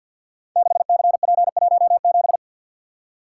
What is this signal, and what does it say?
Morse code, heard as a keyed tone, about 48 wpm, 700 Hz. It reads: 6CP17
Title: Code Groups - 5 Letter:Numbers - 50wpm